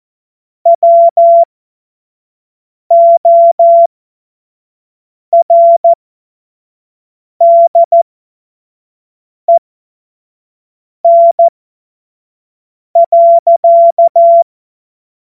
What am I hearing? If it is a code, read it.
Morse code, heard as a keyed tone, about 14 wpm, spaced as for 5 wpm, 680 Hz. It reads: WORDEN.
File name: Morsekurs / Witze / Witz 15. November 2022 25 F70